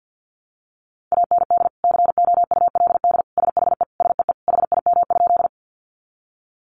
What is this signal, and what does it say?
Morse code, heard as a keyed tone, about 50 wpm, 700 Hz. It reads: AND COULD SHE SEE HIM?
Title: Sentences from Top 100 Words 50wpm